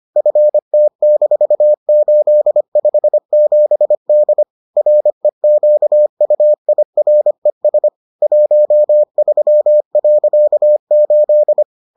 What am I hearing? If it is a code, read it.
Morse code, heard as a keyed tone, about 25 wpm, 600 Hz. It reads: FT-857D REQUIRES 13.8